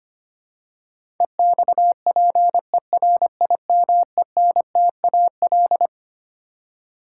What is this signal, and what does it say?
Morse code, heard as a keyed tone, about 25 wpm, 700 Hz. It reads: EXPERIMENTAL
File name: Twelve Letter Words - 25wpm